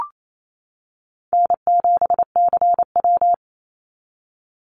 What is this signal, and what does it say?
Morse code, heard as a keyed tone, about 28 wpm, 700 Hz. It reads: N7CW
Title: NA SOTA Chaser Callsigns 28wpm